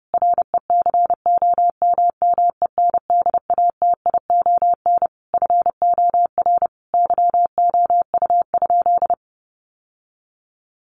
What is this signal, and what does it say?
Morse code, heard as a keyed tone, about 30 wpm, 700 Hz. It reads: RECOMMENDATION FOR YOU?
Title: Sentences from Top 1400 Words 30wpm